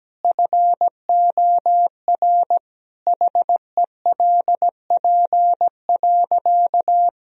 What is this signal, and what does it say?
Morse code, heard as a keyed tone, about 17 wpm, 700 Hz. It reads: FOR HELP.
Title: Sentences from Top 1000 Words - Review Part 3 - 17wpm